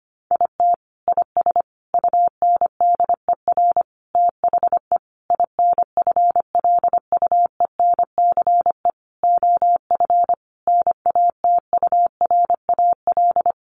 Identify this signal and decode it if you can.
Morse code, heard as a keyed tone, about 25 wpm, 700 Hz. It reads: IT IS UNDER THE INFLUENCE OF NATURAL
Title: Sentences from Top 1400 Words 25wpm